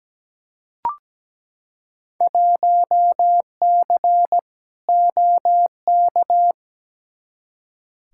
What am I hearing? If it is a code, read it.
Morse code, heard as a keyed tone, about 17 wpm, 700 Hz. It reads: 1C OK
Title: Field Day Exchanges Encore - 17wpm